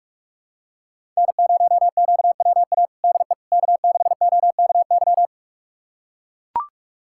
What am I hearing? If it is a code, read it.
Morse code, heard as a keyed tone, about 43 wpm, 700 Hz. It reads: N0QWA DE K6OKY
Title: USA Callsign DE Callsign - 5 Characters - 45wpm